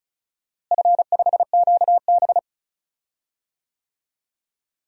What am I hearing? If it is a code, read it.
Morse code, heard as a keyed tone, about 34 wpm, 700 Hz. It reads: F5QB
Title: DXCC-100 Call Signs - 4 Characters - Repeated 3x - 35wpm